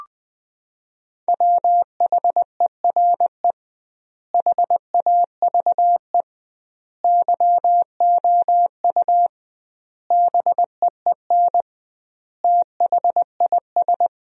Morse code, heard as a keyed tone, about 20 wpm, 700 Hz. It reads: WHERE HAVE YOU BEEN THIS
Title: 2x Word-Spacing - Sentences from Top 300 Words - 20wpm